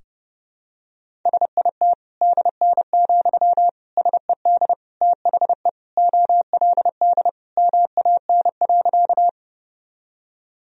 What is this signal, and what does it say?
Morse code, heard as a keyed tone, about 30 wpm, 700 Hz. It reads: SIT DN, SED THE OLD MAN.
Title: Sentences from Top 500 Words and Abbreviations 30wpm